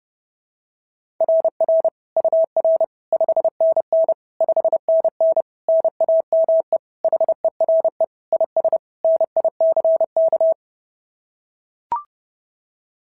Morse code, heard as a keyed tone, about 30 wpm, 690 Hz. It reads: RR UR 5NN 5NN NAME HERE IS NICK
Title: SOTA QSO Practice 30wpm